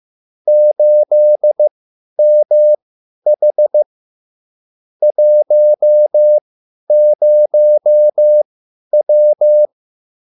Morse code, heard as a keyed tone, about 15 wpm, spaced as for 10 wpm, 600 Hz. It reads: 8MH 10W